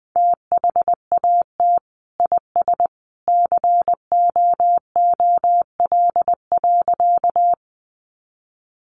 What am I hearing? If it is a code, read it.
Morse code, heard as a keyed tone, about 20 wpm, 700 Hz. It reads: THAT IS COOL.
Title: Sentences from Top 1200 Words 20wpm